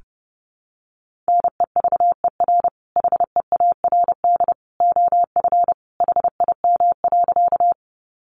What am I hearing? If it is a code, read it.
Morse code, heard as a keyed tone, about 30 wpm, 700 Hz. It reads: NEVER HEARD OF HIM.
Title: Sentences from Top 400 Words 30wpm